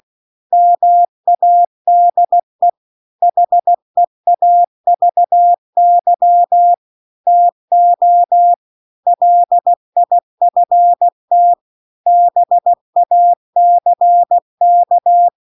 Morse code, heard as a keyed tone, about 16 wpm, 700 Hz. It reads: MADE HEAVY TO LIFT BACK